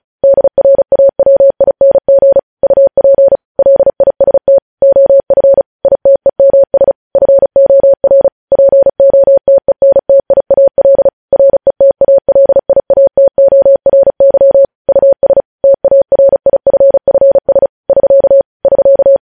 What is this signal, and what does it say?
Morse code, heard as a keyed tone, about 36 wpm, 550 Hz. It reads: DRAWING UP LIST OF ITEMS FOR POTENTIAL RETALIATORY US TARIFFS <SK> <SK>
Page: news headlines in morse code at 35 wpm